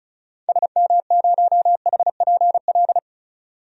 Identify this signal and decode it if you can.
Morse code, heard as a keyed tone, about 36 wpm, 700 Hz. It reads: SM0HPL
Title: DXCC-100 Call Signs - 6 Characters - 35wpm